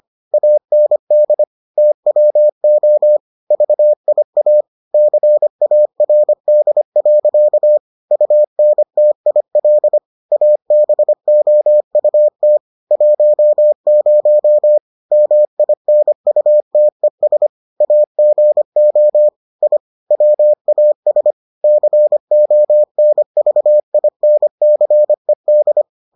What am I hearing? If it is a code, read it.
Morse code, heard as a keyed tone, about 25 wpm, 600 Hz. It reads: AND TWO VIA CARD. UNTIL ABOUT 10 MINUTES AGO I WAS CONVINCED